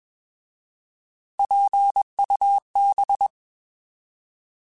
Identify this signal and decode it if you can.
Morse code, heard as a keyed tone, about 21 wpm, 800 Hz. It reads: PUB
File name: Alphabetical / Alphabetical mp3s / Ps @20